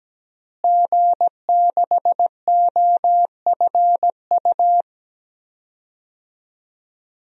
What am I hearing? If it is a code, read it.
Morse code, heard as a keyed tone, about 17 wpm, 700 Hz. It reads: G6OFU
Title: DXCC-100 Call Signs - 5 Characters - Repeated 2x - 17wpm